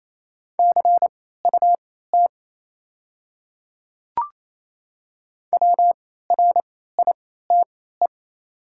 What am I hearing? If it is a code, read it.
Morse code, heard as a keyed tone, about 27 wpm, spaced as for 16 wpm, 700 Hz. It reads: CUT WRITE
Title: 3x Character Spacing - Top 300 Words - 28wpm